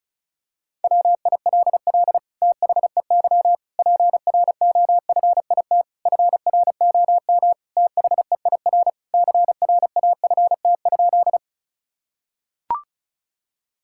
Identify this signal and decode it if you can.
Morse code, heard as a keyed tone, about 36 wpm, 700 Hz. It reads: WILL THEY PROFIT FROM THEIR CRAFT?
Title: Sentences from Top 1000 Words - Review Part 6 - 35wpm